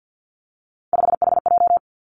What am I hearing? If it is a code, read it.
Morse code, heard as a keyed tone, about 50 wpm, 700 Hz. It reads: 5HJ
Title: Three Challenging Letter-Number Combinations 50wpm